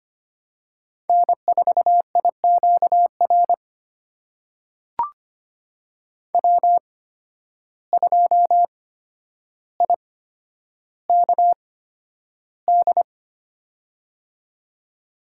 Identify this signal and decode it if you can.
Morse code, heard as a keyed tone, about 25 wpm, spaced as for 11 wpm, 700 Hz. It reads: N4IQR W2IKD
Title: USA Call Signs - 5 Characters - 8x Character Spacing - 25wpm